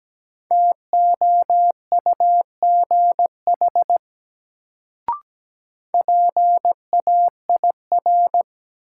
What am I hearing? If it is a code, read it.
Morse code, heard as a keyed tone, about 17 wpm, 700 Hz. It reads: TOUGH PAIR
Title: Sentences from Top 1200 Words - New Word Drill - 17wpm